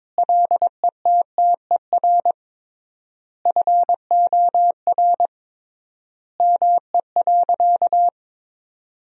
Morse code, heard as a keyed tone, about 21 wpm, 700 Hz. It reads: LETTER FOR ME.
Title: 3x Word-Spacing - Sentences from Top 300 Words - 22wpm